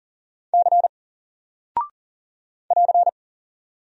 Morse code, heard as a keyed tone, about 40 wpm, 700 Hz. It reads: C <AR>
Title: Single Character (Letter:Number:Punctuation:Prosign) Alt 40wpm